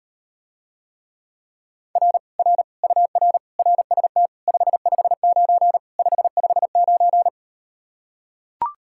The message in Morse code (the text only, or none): R R UR RST 559 559